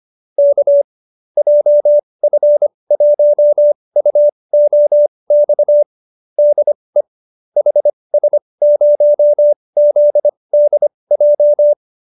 Morse code, heard as a keyed tone, about 25 wpm, spaced as for 20 wpm, 590 Hz. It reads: K JF1UOX DE HS0ZDJ